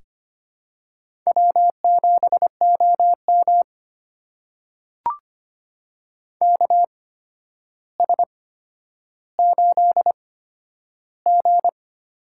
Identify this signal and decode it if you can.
Morse code, heard as a keyed tone, about 25 wpm, spaced as for 11 wpm, 700 Hz. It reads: W7OM KS8G
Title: USA Call Signs - 4 Characters - 8x Character Spacing - 25wpm